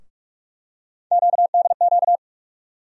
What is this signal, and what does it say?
Morse code, heard as a keyed tone, about 45 wpm, 700 Hz. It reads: QDQ
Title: USA Call Sign Postfixes - 3 Letters 45wpm